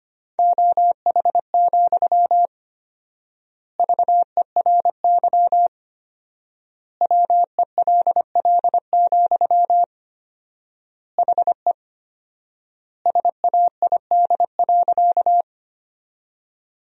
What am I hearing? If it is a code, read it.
Morse code, heard as a keyed tone, about 25 wpm, 700 Hz. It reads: OH, VERY WELL, HE SAID.
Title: 4x Word-Spacing - Sentences from Top 500 Words - 25wpm